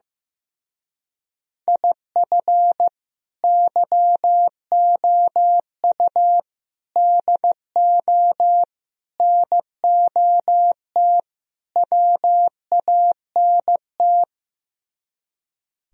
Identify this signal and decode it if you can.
Morse code, heard as a keyed tone, about 15 wpm, 700 Hz. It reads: IF YOU DO NOT WANT
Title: Speed-Racing - Sets of 5 Words 15wpm